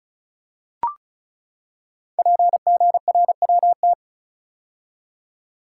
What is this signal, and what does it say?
Morse code, heard as a keyed tone, about 36 wpm, 700 Hz. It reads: PGRWT
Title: Code Groups - 5 Letters - 35wpm